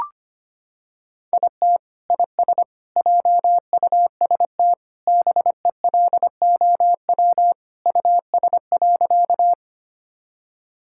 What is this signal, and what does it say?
Morse code, heard as a keyed tone, about 25 wpm, 700 Hz. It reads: IT IS JUST BELOW US.
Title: Sentences from Top 1000 Words - Review Part 5 - 25wpm